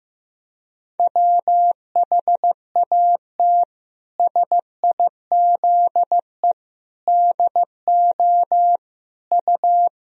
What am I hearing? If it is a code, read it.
Morse code, heard as a keyed tone, about 15 wpm, 700 Hz. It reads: WHAT SIZE DO U
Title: Sentences from Top 400 Words and Abbreviations 15wpm